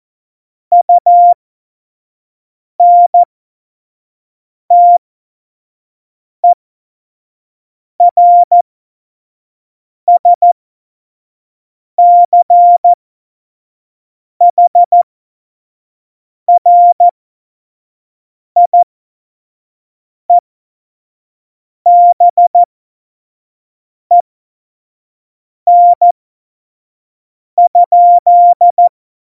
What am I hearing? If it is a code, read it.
Morse code, heard as a keyed tone, about 14 wpm, spaced as for 5 wpm, 700 Hz. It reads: UNTERSCHRIEBEN?
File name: Morsekurs / Witze / Witz 16. Juli 2019, 25 F70